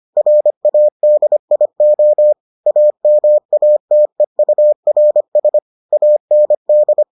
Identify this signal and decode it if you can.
Morse code, heard as a keyed tone, about 25 wpm, 600 Hz. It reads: RADIO AMATEURS AND